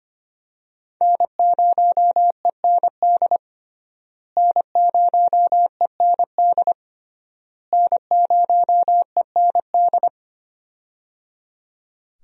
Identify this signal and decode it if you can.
Morse code, heard as a keyed tone, about 25 wpm, 700 Hz. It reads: N0END N0END N0END
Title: USA Call Signs - 5 Characters - Repeated 3x - 25wpm